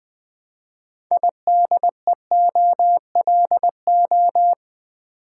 Morse code, heard as a keyed tone, about 20 wpm, 700 Hz. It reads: IDEOLO
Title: Build A Word - 2 Letters At A Time - 8-Letter Words - 20wpm